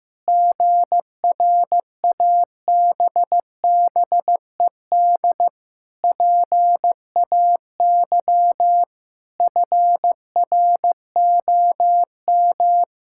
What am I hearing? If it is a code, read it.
Morse code, heard as a keyed tone, about 15 wpm, 700 Hz. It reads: GRABBED PAY FROM
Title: Pangram Challenge v2 - 15wpm